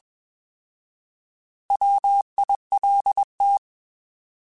Morse code, heard as a keyed tone, about 21 wpm, 800 Hz. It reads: WILT
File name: Alphabetical / Alphabetical mp3s / Ws @20